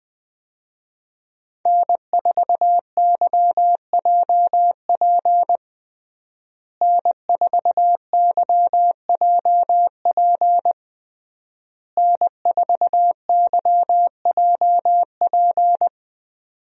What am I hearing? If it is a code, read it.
Morse code, heard as a keyed tone, about 21 wpm, 700 Hz. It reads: N4YJP N4YJP N4YJP
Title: USA Call Signs - 5 Characters - Repeated 3x - 20wpm